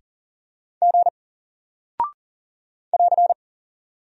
Morse code, heard as a keyed tone, about 40 wpm, 700 Hz. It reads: G <AR>
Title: Single Character (Letter:Number:Punctuation:Prosign) Alt 40wpm